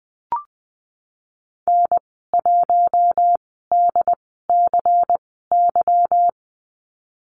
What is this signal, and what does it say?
Morse code, heard as a keyed tone, about 20 wpm, spaced as for 15 wpm, 700 Hz. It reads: N1DCY
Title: 154 Lesson 25g - Letter p - Call Signs 20wpm